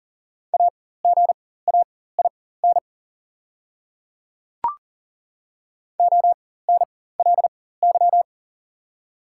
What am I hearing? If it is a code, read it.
Morse code, heard as a keyed tone, about 40 wpm, spaced as for 19 wpm, 700 Hz. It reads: AGAIN ONLY
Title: 4x Character Spacing - Top 200 Words - 40wpm